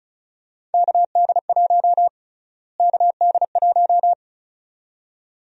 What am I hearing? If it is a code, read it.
Morse code, heard as a keyed tone, about 36 wpm, 700 Hz. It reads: KD1 KD1
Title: USA Call Sign Prefixes - Repeated 2x - 35wpm